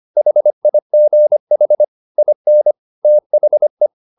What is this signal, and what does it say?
Morse code, heard as a keyed tone, about 25 wpm, 600 Hz. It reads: HIGH IN THE